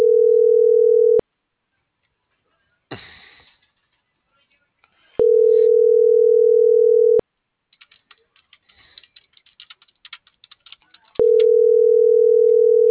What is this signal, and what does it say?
Ringback tone (North America), 440 + 480 Hz.